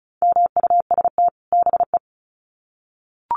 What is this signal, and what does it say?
Morse code, heard as a keyed tone, about 34 wpm, 700 Hz. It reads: MUST BE
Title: Sets of 2 Words 35wpm